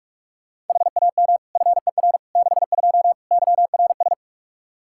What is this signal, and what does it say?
Morse code, heard as a keyed tone, about 46 wpm, 700 Hz. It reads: HAM FER 62 YRS
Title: Practice QSO - Exchange 3 - More Info - 45wpm